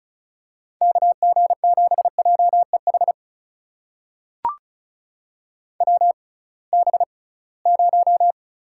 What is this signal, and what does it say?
Morse code, heard as a keyed tone, about 36 wpm, spaced as for 23 wpm, 700 Hz. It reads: KG7JEH WB0
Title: USA Call Signs - 6 Characters - 6x Character Spacing - 35wpm